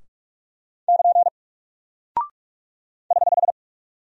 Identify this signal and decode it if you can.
Morse code, heard as a keyed tone, about 45 wpm, 700 Hz. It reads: <KN> <HH>
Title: Single Character (Letter:Number:Punctuation:Prosign) Alt 45wpm